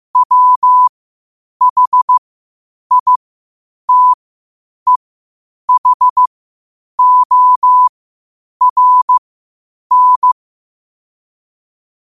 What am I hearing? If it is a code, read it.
Morse code, heard as a keyed tone, about 15 wpm, spaced as for 8 wpm, 1000 Hz. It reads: WHITEHORN
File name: qotdfiles / 8wpm